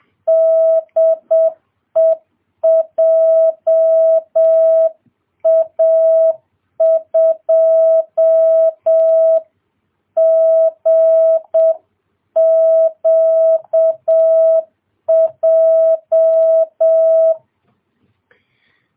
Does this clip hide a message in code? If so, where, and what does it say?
Yes: Morse code from 0.3 to 17.3 s, heard as a keyed tone, about 7 wpm, 650 Hz. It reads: DE JA2GQJ